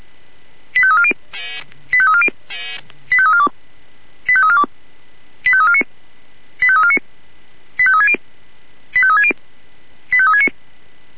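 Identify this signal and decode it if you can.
Five-tone selective call (ZVEI), heard as quick runs of five tones, tones of about 70 ms. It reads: 85340 85340 85342 85342 85349 85349 85360 85360 85369
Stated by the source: ZVEI five-tone selective call received by SDR.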